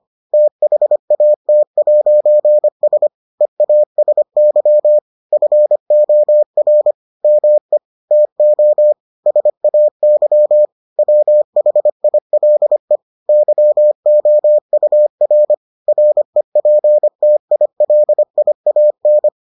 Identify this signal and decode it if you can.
Morse code, heard as a keyed tone, about 25 wpm, 600 Hz. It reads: THAT'S EASY FOR ME TO SAY WHILE YOUR REPTILIAN